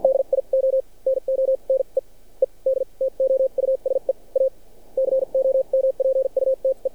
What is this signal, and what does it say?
Morse code, heard as a keyed tone, about 59 wpm, spaced as for 39 wpm, 530 Hz. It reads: DIO NO NEED TO USE A COMPUTE
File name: archived / CRSnet